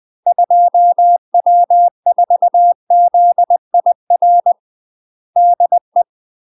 Morse code, heard as a keyed tone, about 20 wpm, 700 Hz. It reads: 2W4ZIR DE